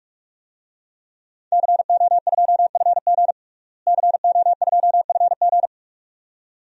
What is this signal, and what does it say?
Morse code, heard as a keyed tone, about 46 wpm, 700 Hz. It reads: CO2FG CO2FG
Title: DXCC-100 Call Signs - 5 Characters - Repeated 2x - 45wpm